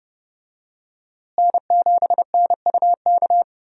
Morse code, heard as a keyed tone, about 29 wpm, 700 Hz. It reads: N7NUK
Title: USA Call Signs - 5 Characters - Repeated 3x - 30wpm